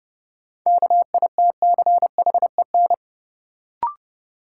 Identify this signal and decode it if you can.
Morse code, heard as a keyed tone, about 30 wpm, 700 Hz. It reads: KITCHEN
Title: Sentences from Top 1400 Words - New Word Drill - 30wpm